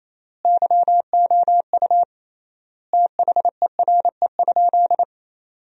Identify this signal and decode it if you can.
Morse code, heard as a keyed tone, about 28 wpm, 700 Hz. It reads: YOU THERE?
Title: 3x Word-Spacing - Sentences from Top 100 Words - 28wpm